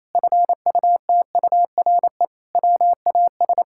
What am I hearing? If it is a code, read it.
Morse code, heard as a keyed tone, about 28 wpm, 700 Hz. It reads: FUTURE WAS
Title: Sentences from Top 1000 Words - Review Part 6 - 28wpm